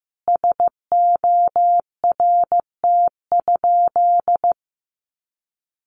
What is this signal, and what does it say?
Morse code, heard as a keyed tone, about 15 wpm, 700 Hz. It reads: SORT?
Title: Sentences from Top 1300 Words 15wpm